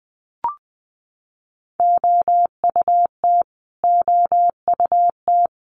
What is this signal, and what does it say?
Morse code, heard as a keyed tone, about 20 wpm, 700 Hz. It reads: OUT OUT
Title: Top 500 Words Repeated 20wpm